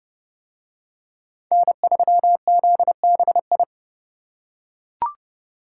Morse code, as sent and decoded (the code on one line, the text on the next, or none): -. ...-- --.. -... ..
N3ZBI